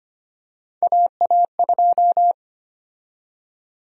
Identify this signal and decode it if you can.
Morse code, heard as a keyed tone, about 25 wpm, 700 Hz. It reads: AA2
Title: USA Call Sign Prefixes - Repeated 2x - 25wpm